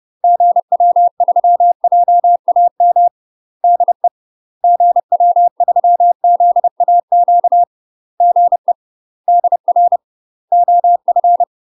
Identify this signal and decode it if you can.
Morse code, heard as a keyed tone, about 30 wpm, 700 Hz. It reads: GW3JAM DE GW3ZAQ GE DR OF
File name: UK MASTER QSOs TRAINING / 30WPM / UK MASTER QSOs 0019